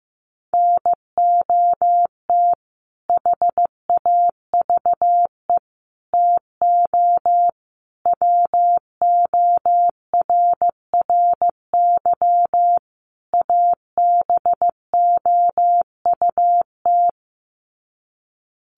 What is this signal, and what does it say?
Morse code, heard as a keyed tone, about 15 wpm, 700 Hz. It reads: NOT HAVE TO WORRY ABOUT
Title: Sets of 5 Words 15wpm